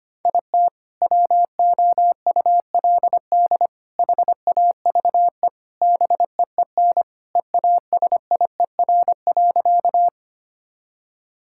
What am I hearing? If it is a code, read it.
Morse code, heard as a keyed tone, about 25 wpm, 700 Hz. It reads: IT WOULD HAVE BEEN EASIER.
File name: practice / 25wpm / Sentences with 4-Word Phrases - 25wpm